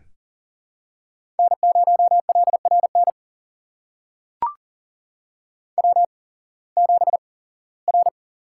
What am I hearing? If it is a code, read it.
Morse code, heard as a keyed tone, about 40 wpm, spaced as for 21 wpm, 700 Hz. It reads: N0LRN W7R
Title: USA Call Signs - 5 Characters - 8x Character Spacing - 40wpm